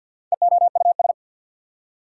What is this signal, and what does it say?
Morse code, heard as a keyed tone, about 48 wpm, 690 Hz. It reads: EOUS
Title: Prefixes and Suffixes 50wpm